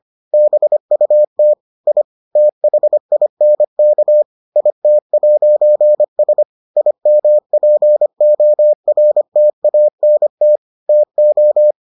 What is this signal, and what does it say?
Morse code, heard as a keyed tone, about 25 wpm, 600 Hz. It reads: BUT I THINK IT'S IMPORTANT TO